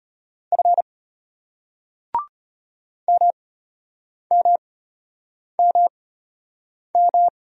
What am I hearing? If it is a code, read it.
Morse code, heard as a keyed tone, about 34 wpm, spaced as for 8 wpm, 700 Hz. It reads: F MMMM